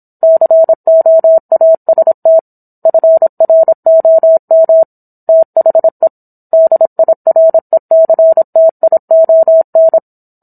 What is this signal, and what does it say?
Morse code, heard as a keyed tone, about 27 wpm, 650 Hz. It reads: COAST FROM THE DIRECTION